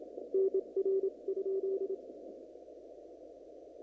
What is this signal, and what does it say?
Morse code, heard as a keyed tone, about 29 wpm, spaced as for 24 wpm, 390 Hz. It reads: NR?
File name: CW / kt5j wpx cw-op-k5pi